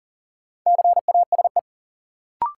Morse code, heard as a keyed tone, about 40 wpm, 700 Hz. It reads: CASE
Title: Sets of 2 Words - New Word Drill - 40wpm